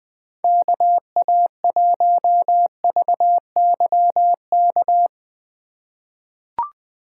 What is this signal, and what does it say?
Morse code, heard as a keyed tone, about 20 wpm, 700 Hz. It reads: KA1VYK
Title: USA Call Signs - 2x Character Spacing - 20wpm